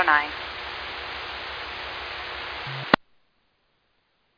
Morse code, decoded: none

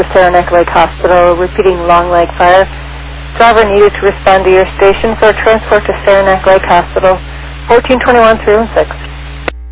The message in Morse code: none

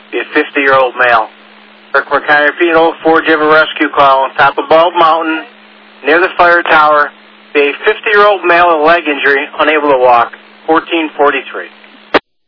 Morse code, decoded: none

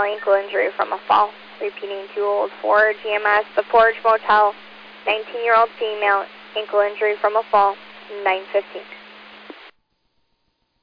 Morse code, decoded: none

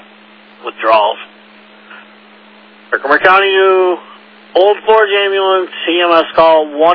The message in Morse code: none

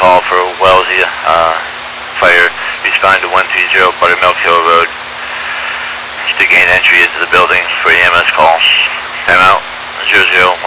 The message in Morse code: none

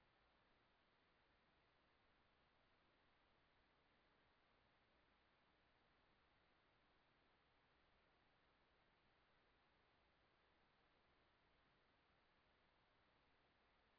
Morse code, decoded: none

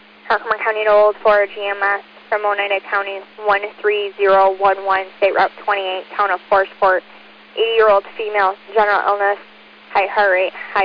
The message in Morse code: none

